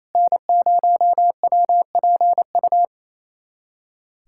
Morse code, heard as a keyed tone, about 28 wpm, 700 Hz. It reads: N0WPU
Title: USA Callsign DE Callsign 28wpm